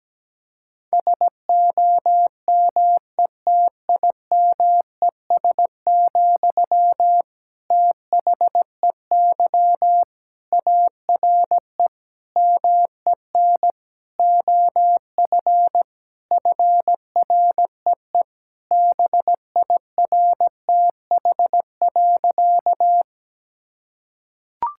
Morse code, heard as a keyed tone, about 17 wpm, 700 Hz. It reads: SOMETIMES, THEY ARE MEN OF FREE BIRTH.
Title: Sentences from Top 1000 Words - Review Part 5 - 17wpm